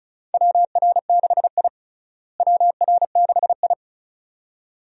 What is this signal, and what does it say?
Morse code, heard as a keyed tone, about 34 wpm, 700 Hz. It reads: WR6I WR6I
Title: USA Call Signs - 4 Characters - Repeated 2x - 35wpm